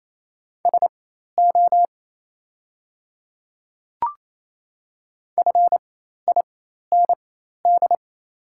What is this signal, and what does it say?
Morse code, heard as a keyed tone, about 27 wpm, spaced as for 13 wpm, 700 Hz. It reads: SO FIND
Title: 4x Character Spacing - Top 100 Words - 28wpm